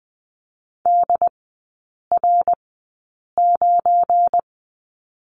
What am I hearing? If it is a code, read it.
Morse code, heard as a keyed tone, about 20 wpm, spaced as for 8 wpm, 690 Hz. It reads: DR9
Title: 205 Lesson 33e - Letter j - Three Characters 20wpm